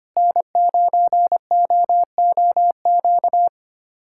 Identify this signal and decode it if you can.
Morse code, heard as a keyed tone, about 25 wpm, 700 Hz. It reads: N9OOQ
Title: USA Callsign DE Callsign 25wpm